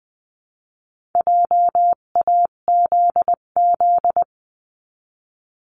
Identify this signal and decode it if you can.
Morse code, heard as a keyed tone, about 20 wpm, 700 Hz. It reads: JAZZ
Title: 258 Lesson 42c - Words 16-20wpm